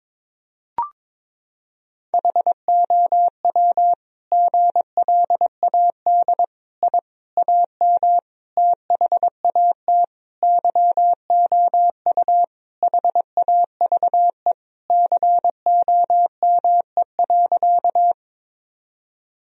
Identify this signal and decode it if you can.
Morse code, heard as a keyed tone, about 22 wpm, 700 Hz. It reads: HOW GLAD I AM THAT YOU HAVE COME.
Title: Sentences from Top 1000 Words - Review Part 2 - 22wpm